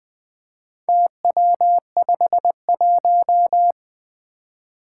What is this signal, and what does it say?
Morse code, heard as a keyed tone, about 20 wpm, 700 Hz. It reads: TW51
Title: Four Challenging Letter-Number Combinations 20wpm